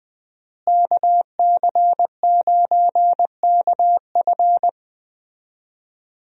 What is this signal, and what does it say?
Morse code, heard as a keyed tone, about 20 wpm, 700 Hz. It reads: KC9KF
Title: USA Call Signs - 5 Characters - Repeated 2x - 20wpm